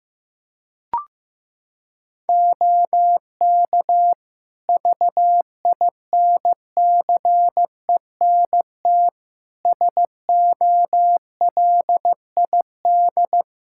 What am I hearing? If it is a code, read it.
Morse code, heard as a keyed tone, about 15 wpm, 700 Hz. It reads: OK VINCENT SOLID
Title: Practice QSO - Exchange 3 - More Info - 15wpm